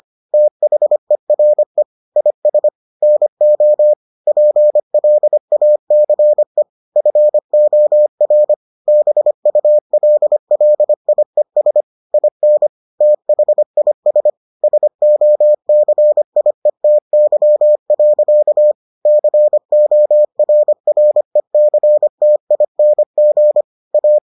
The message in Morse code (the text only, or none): THERE IS NO PLACE FOR BULLIES IN THIS SOCIETY. CORRECTING A